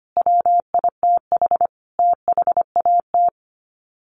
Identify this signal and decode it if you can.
Morse code, heard as a keyed tone, about 25 wpm, 700 Hz. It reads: WITH THAT
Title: Sets of 5 Words Encore 25wpm